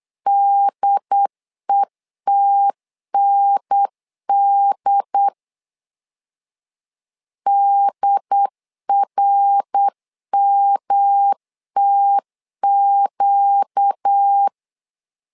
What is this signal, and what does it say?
Morse code, heard as a keyed tone, about 8 wpm, 800 Hz. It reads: DETND DRMTQ
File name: LEZIONI / 4 Q1 8wpm 3035